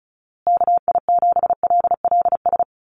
Morse code, heard as a keyed tone, about 34 wpm, 700 Hz. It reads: KI7LLS
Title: USA Call Signs Repeated 3x 35wpm